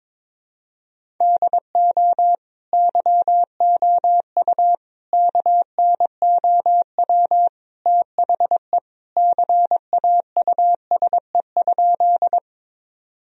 Morse code, heard as a keyed tone, about 22 wpm, 700 Hz. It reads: DO YOU KNOW THE CAUSE?
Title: Sentences from Top 1000 Words - Review Part 1 - 22wpm